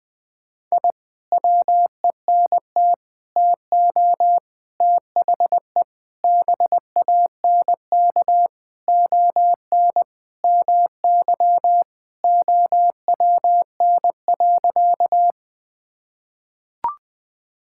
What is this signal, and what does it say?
Morse code, heard as a keyed tone, about 20 wpm, 700 Hz. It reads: I WENT TO THE BANK ON MY OWN.
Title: Sentences from Top 1000 Words - Review Part 6 - 20wpm